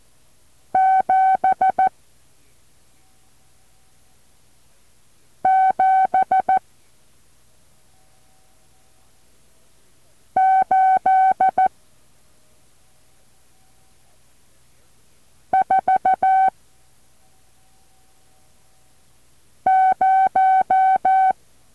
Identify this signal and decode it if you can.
Morse code, heard as a keyed tone, about 14 wpm, 760 Hz. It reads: 7 7 8 4 0
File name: intl morse code2